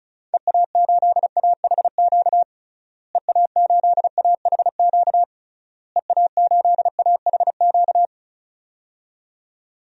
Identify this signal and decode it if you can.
Morse code, heard as a keyed tone, about 36 wpm, 700 Hz. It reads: EA8AHQ EA8AHQ EA8AHQ
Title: DXCC-100 Call Signs - 6 Characters - Repeated 3x - 35wpm